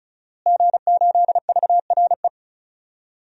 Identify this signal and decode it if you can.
Morse code, heard as a keyed tone, about 34 wpm, 700 Hz. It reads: G8VRE
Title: DXCC-100 Call Signs - 5 Characters - Repeated 3x - 35wpm